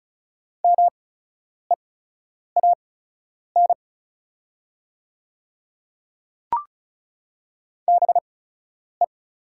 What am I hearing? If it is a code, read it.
Morse code, heard as a keyed tone, about 36 wpm, spaced as for 10 wpm, 690 Hz. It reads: MEAN BE
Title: 8x Character Spacing - Top 200 Words - 35wpm